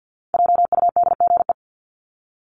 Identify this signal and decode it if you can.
Morse code, heard as a keyed tone, about 50 wpm, 700 Hz. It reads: JUDGE
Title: Top 1751-2000 Words 50wpm